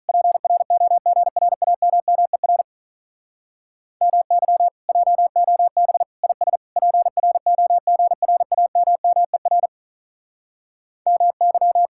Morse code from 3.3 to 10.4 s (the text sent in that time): MY JOB IS PROGRAMMER